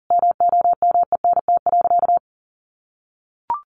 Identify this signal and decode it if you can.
Morse code, heard as a keyed tone, about 40 wpm, 700 Hz. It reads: MOMENT.
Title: Sentences from Top 1400 Words 40wpm